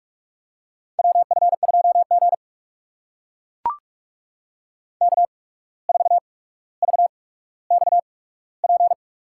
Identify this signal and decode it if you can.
Morse code, heard as a keyed tone, about 44 wpm, spaced as for 18 wpm, 700 Hz. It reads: WF2G K4VXP
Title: USA Call Signs - 8x Character Spacing - 45wpm